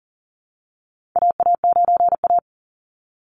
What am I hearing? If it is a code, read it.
Morse code, heard as a keyed tone, about 40 wpm, 700 Hz. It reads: AA9A
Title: Call Signs Repeated 2x - 40wpm